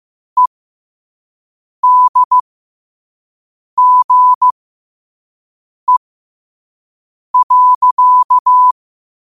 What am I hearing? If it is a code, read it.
Morse code, heard as a keyed tone, about 15 wpm, spaced as for 5 wpm, 1000 Hz. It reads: EDGE.
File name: qotdfiles / 5wpm